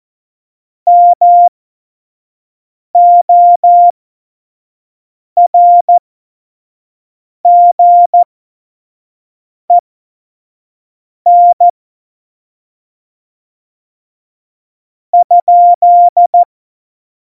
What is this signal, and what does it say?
Morse code, heard as a keyed tone, about 14 wpm, spaced as for 5 wpm, 700 Hz. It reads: MORGEN ?
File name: Morsekurs / Witze / Witz 24.Juli 2018, 25 F70